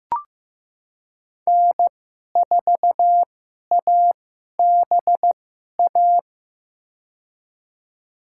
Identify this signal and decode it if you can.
Morse code, heard as a keyed tone, about 15 wpm, spaced as for 11 wpm, 700 Hz. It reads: N4ABA
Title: USA Call Signs - 5 Characters - 2x Character Spacing - 15wpm